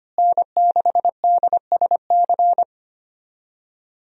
Morse code, heard as a keyed tone, about 25 wpm, 700 Hz. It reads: N6DSC
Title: USA Call Signs - 5 Characters - Repeated 2x - 25wpm